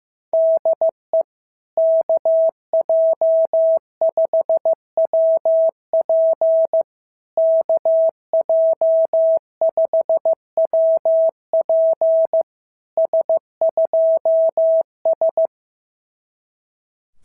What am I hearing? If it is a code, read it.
Morse code, heard as a keyed tone, about 15 wpm, 650 Hz. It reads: DE KJ5WP KJ5WP S2S
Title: SOTA QSO Practice 15wpm